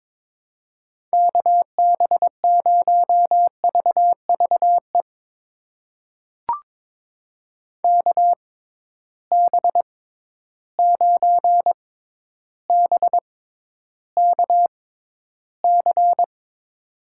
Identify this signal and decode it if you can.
Morse code, heard as a keyed tone, about 22 wpm, spaced as for 11 wpm, 700 Hz. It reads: KB0VVE KB9BKC
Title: USA Call Signs - 6 Characters - 6x Character Spacing - 22wpm